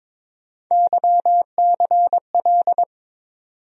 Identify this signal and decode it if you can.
Morse code, heard as a keyed tone, about 22 wpm, 700 Hz. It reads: YCL
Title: USA Call Sign Postfixes - 3 Letters 22wpm